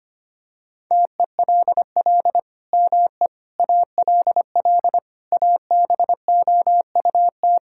Morse code, heard as a keyed tone, about 25 wpm, 700 Hz. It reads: TELL ME ALL ABOUT
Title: Sentences from Top 1000 Words - Review Part 6 - 25wpm